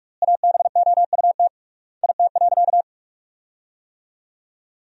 Morse code, heard as a keyed tone, about 46 wpm, 700 Hz. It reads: ABOUT IT.
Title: 3x Word-Spacing - Sentences from Top 200 Words - 45wpm